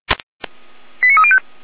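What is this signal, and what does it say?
Five-tone selective call (ZVEI), heard as quick runs of five tones, tones of about 70 ms. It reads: 89305
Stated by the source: ZVEI five-tone call 89187 received by SDR, from the Feuerwehr Orlamünde audio server.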